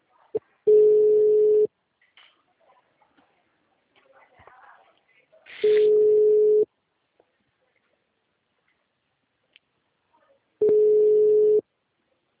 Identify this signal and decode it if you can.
Ringback tone (Europe, 425 Hz), with speech.
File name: out- -2003- - - .152